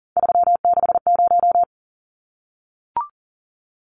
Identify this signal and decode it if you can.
Morse code, heard as a keyed tone, about 40 wpm, 700 Hz. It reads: 360
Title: Numbers - 3-Digit 40wpm